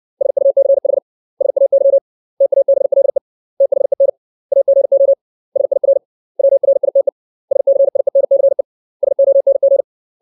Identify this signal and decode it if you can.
Morse code, heard as a keyed tone, about 60 wpm, 550 Hz. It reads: SUCH SAY TABLE THEN AGO SIR WRITE SCIENCE SONG
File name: 632 words / 632 Words 60WPM